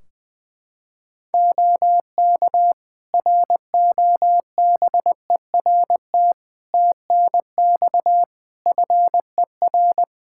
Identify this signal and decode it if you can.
Morse code, heard as a keyed tone, about 20 wpm, 700 Hz. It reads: OK ROBERT TNX FER
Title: K1USN SST QSO Practice 20wpm